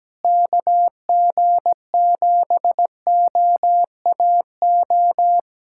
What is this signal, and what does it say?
Morse code, heard as a keyed tone, about 17 wpm, 700 Hz. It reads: KG7OAO
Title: USA Callsign DE Callsign - 6 Characters - 17wpm